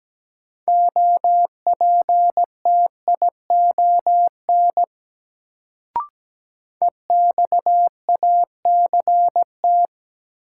Morse code, heard as a keyed tone, about 17 wpm, 700 Hz. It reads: OPTION EXACT
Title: Sentences from Top 1300 Words - New Word Drill Alt - 17wpm